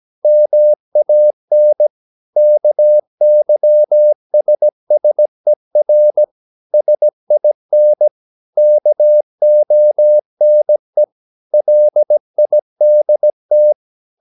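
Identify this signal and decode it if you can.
Morse code, heard as a keyed tone, about 17 wpm, 600 Hz. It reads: MAN KYSSER SIN KONE LIDT
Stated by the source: Slow (QRS) Morse practice at 17 wpm, from CW med Gnister.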